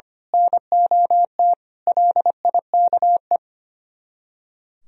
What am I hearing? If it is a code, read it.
Morse code, heard as a keyed tone, about 28 wpm, 700 Hz. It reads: NOT LIKE
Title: Speed-Racing - Sets of 5 Words 25wpm